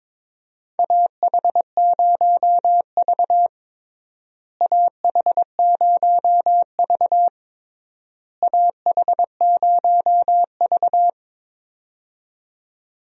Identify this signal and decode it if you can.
Morse code, heard as a keyed tone, about 22 wpm, 700 Hz. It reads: AH0V AH0V AH0V
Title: USA Call Signs - 4 Characters - Repeated 3x - 22wpm